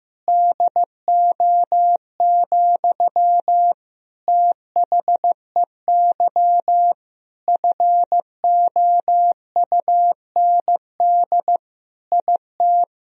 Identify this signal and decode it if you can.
Morse code, heard as a keyed tone, about 15 wpm, 700 Hz. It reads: DO, THEY FOUND IT
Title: Sentences from Top 1000 Words - Review Part 4 - 15wpm